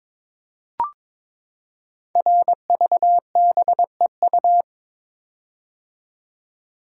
Morse code, heard as a keyed tone, about 22 wpm, 700 Hz. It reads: RVBEU
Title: Code Groups - 5 Letters - 22wpm